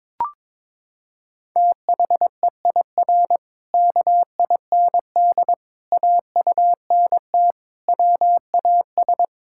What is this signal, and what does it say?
Morse code, heard as a keyed tone, about 22 wpm, 700 Hz. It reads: THEIR KIND AUNT WAS
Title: Pangram Challenge v2 - 22wpm